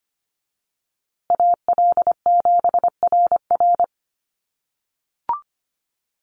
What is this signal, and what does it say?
Morse code, heard as a keyed tone, about 25 wpm, 700 Hz. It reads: AL7RR
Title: USA Call Signs Repeated 3x 25wpm